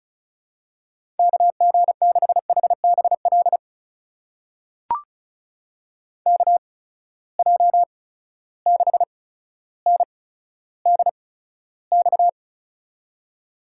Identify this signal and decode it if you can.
Morse code, heard as a keyed tone, about 36 wpm, spaced as for 15 wpm, 700 Hz. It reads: KG6HBL KJ6NDX
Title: USA Call Signs - 6 Characters - 8x Character Spacing - 35wpm